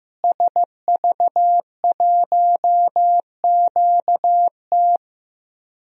Morse code, heard as a keyed tone, about 15 wpm, 700 Hz. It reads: SV1QT
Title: DXCC-100 Call Signs - 5 Characters - Repeated 3x - 15wpm